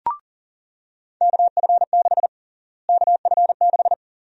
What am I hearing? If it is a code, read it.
Morse code, heard as a keyed tone, about 40 wpm, 700 Hz. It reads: KF6 KF6
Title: USA Call Sign Prefixes - Repeated 2x - 40wpm